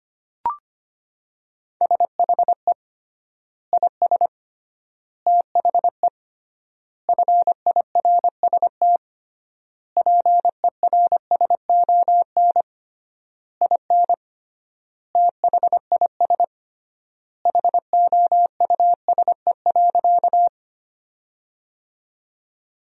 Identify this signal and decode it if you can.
Morse code, heard as a keyed tone, about 25 wpm, 700 Hz. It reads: SHE IS THE FIRST PERSON IN THIS HOUSE.
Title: 3x Word-Spacing - Sentences from Top 400 Words - 25wpm